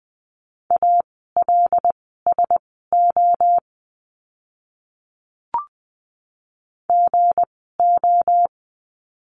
Morse code, heard as a keyed tone, about 20 wpm, spaced as for 14 wpm, 700 Hz. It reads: ALSO GO
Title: 237 Lesson 38f - Letter q - Words 20wpm